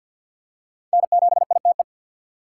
Morse code, heard as a keyed tone, about 46 wpm, 700 Hz. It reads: N7ITE
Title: USA Call Signs - 5 Characters - 4x Character Spacing - 50wpm